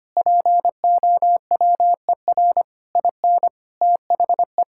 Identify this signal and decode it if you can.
Morse code, heard as a keyed tone, about 25 wpm, 700 Hz. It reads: POWER IN THE
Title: Sentences from Top 1000 Words - Review Part 6 - 25wpm